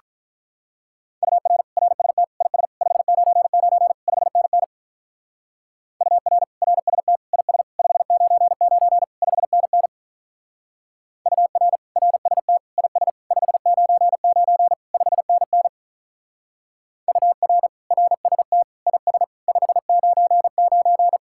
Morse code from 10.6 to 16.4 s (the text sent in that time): UR RST IS 599 5NN